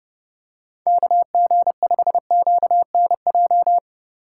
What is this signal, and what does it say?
Morse code, heard as a keyed tone, about 31 wpm, 700 Hz. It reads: KG5QNJ
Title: USA Call Signs - 8x Character Spacing - 30wpm